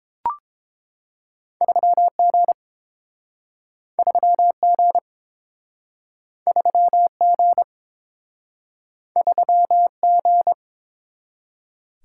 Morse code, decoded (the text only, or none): 3G 3G 3G 3G